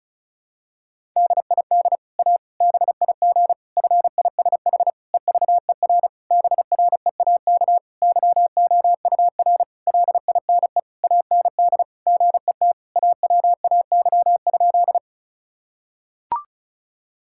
Morse code, heard as a keyed tone, about 36 wpm, 700 Hz. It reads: DID A BIG FISH EVER BREAK YOUR LINE AND GET AWAY?
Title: Sentences from Top 1000 Words - Review Part 6 - 35wpm